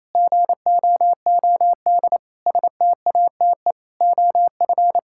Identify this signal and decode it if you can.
Morse code, heard as a keyed tone, about 28 wpm, 700 Hz. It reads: GOOD STATE OF
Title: Sentences from Top 1000 Words - Review Part 5 - 28wpm